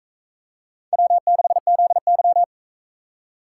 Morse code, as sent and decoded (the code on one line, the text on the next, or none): .-- -.... --.. -.--
W6ZY